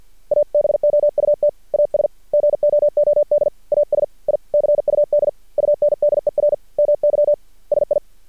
Morse code, heard as a keyed tone, about 48 wpm, 590 Hz. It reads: ABOUT AS GOOD AS I CUD UNDER MY SI
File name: archived / CRSnet